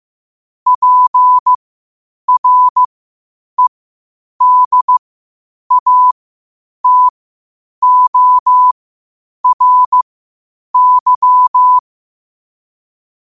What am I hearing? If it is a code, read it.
Morse code, heard as a keyed tone, about 15 wpm, spaced as for 8 wpm, 1000 Hz. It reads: PREDATORY
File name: qotdfiles / 8wpm